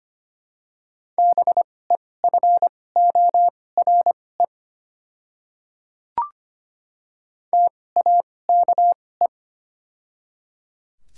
Morse code, heard as a keyed tone, about 25 wpm, spaced as for 18 wpm, 700 Hz. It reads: BEFORE TAKE
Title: 2x Character Spacing - Top 200 Words - 25wpm